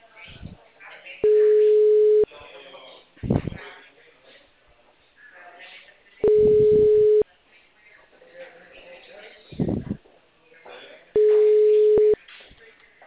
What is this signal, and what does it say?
Ringback tone (Europe, 425 Hz), with speech.
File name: out- -2014- - - .37600